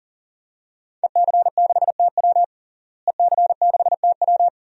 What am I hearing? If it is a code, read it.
Morse code, heard as a keyed tone, about 37 wpm, 700 Hz. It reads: EC6TW EC6TW
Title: DXCC-100 Call Signs - 5 Characters - Repeated 3x - 40wpm